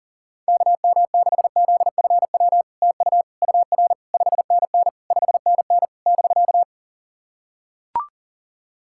Morse code, heard as a keyed tone, about 40 wpm, 700 Hz. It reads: KM6ZFW TU UR 5NN 5NN <BK>
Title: SOTA QSO Practice 40wpm